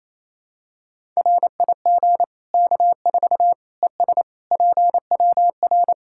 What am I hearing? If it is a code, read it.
Morse code, heard as a keyed tone, about 27 wpm, 700 Hz. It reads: RIG K4 ES PWR
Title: K1USN SST QSO Practice 28wpm